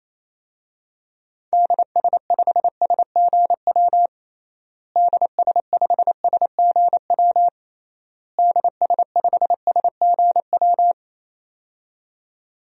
Morse code, heard as a keyed tone, about 27 wpm, 700 Hz. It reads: DS5SGW DS5SGW DS5SGW
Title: DXCC-100 Call Signs - 6 Characters - Repeated 3x - 28wpm